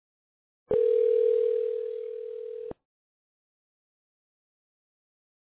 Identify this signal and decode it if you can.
Ringback tone (North America), 440 + 480 Hz.